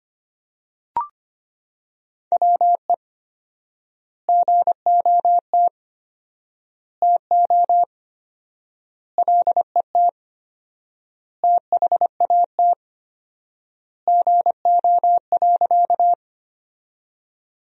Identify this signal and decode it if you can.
Morse code, heard as a keyed tone, about 25 wpm, 700 Hz. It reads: WE GOT TO LET THAT GO.
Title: 4x Word-Spacing - Sentences from Top 300 Words - 25wpm